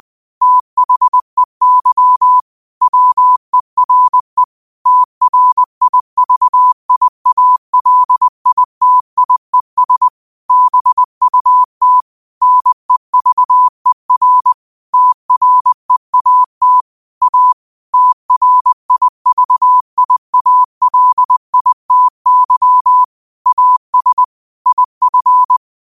Morse code, heard as a keyed tone, about 20 wpm, 1000 Hz. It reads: THEY WERE TRIVIALITIES BUT NEVER TREAT A TRIVIALITY AS IF
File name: qotdfiles / 20wpm